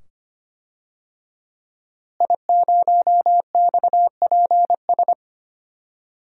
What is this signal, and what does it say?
Morse code, heard as a keyed tone, about 25 wpm, 700 Hz. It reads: I0XPS
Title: DXCC-100 Call Signs - 5 Characters - Repeated 2x - 25wpm